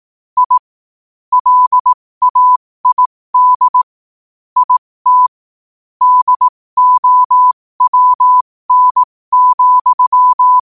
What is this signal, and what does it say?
Morse code, heard as a keyed tone, about 18 wpm, spaced as for 15 wpm, 1000 Hz. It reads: I LAID IT DOWN,
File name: qotdfiles / 15wpm